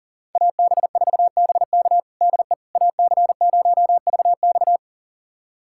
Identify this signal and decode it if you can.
Morse code, heard as a keyed tone, about 37 wpm, 700 Hz. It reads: AB4BK DE AC0VX
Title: USA Callsign DE Callsign - 5 Characters - 40wpm